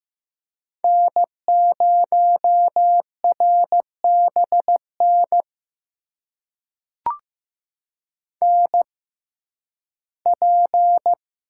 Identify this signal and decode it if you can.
Morse code, heard as a keyed tone, about 15 wpm, spaced as for 11 wpm, 700 Hz. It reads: N0RBN NP
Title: USA Call Signs - 5 Characters - 6x Character Spacing - 15wpm